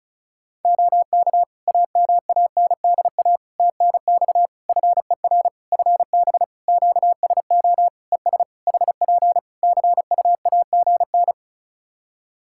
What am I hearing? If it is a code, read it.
Morse code, heard as a keyed tone, about 36 wpm, 700 Hz. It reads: OK AMANDA TNX FER FB QSO ES HP CUAGN
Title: K1USN SST QSO Practice 35wpm